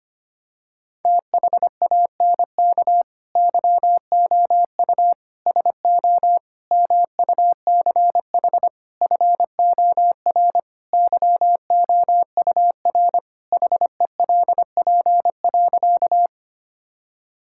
Morse code, heard as a keyed tone, about 25 wpm, 700 Hz. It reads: THANK YOU SO MUCH FOR YOUR HELP.
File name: practice / 25wpm / Sentences with 4-Word Phrases - 25wpm